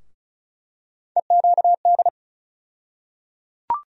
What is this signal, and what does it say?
Morse code, heard as a keyed tone, about 36 wpm, 700 Hz. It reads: EQD
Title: USA Call Sign Postfixes - 3 Letters 35wpm